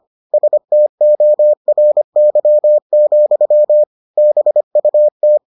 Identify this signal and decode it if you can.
Morse code, heard as a keyed tone, about 25 wpm, 600 Hz. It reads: STORY, BUT